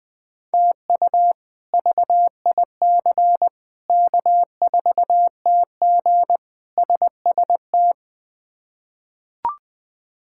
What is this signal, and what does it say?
Morse code, heard as a keyed tone, about 20 wpm, 700 Hz. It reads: TU VIC K4TG SST
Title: K1USN SST QSO Practice - 20wpm